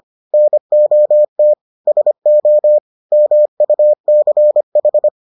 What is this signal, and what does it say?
Morse code, heard as a keyed tone, about 25 wpm, 600 Hz. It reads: NOT SO MUCH